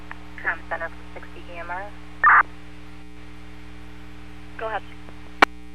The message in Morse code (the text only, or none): none